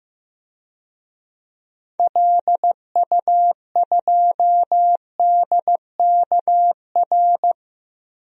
Morse code, heard as a keyed tone, about 15 wpm, 700 Hz. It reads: LU2DKR
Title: DXCC-100 Call Signs - 6 Characters - Repeated 2x - 15wpm